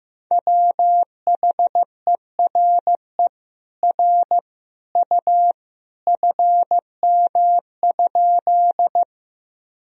Morse code, heard as a keyed tone, about 15 wpm, 700 Hz. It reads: WHERE R U FM?
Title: Sentences from Top 200 Words and Abbreviations 15wpm